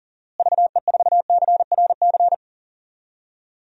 Morse code, heard as a keyed tone, about 40 wpm, 700 Hz. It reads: VE4CRC
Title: DXCC-100 Call Signs - 6 Characters - Repeated 2x - 40wpm